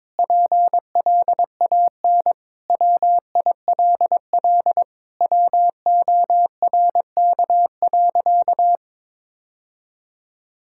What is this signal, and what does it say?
Morse code, heard as a keyed tone, about 22 wpm, 700 Hz. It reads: PLAN WILL WORK.
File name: practice / 22wpm / Sentences with 4-Word Phrases - 22wpm